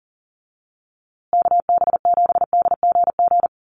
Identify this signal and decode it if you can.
Morse code, heard as a keyed tone, about 40 wpm, 700 Hz. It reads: KB7DGG
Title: USA Call Signs Repeated 2x 40wpm